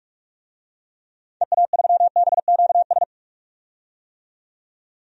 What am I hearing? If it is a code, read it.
Morse code, heard as a keyed tone, about 46 wpm, 700 Hz. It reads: EA3BQS
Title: DXCC-100 Call Signs - 6 Characters - 45wpm